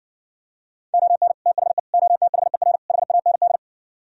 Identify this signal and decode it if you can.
Morse code, heard as a keyed tone, about 69 wpm, 700 Hz. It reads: ON THE OTHER HAND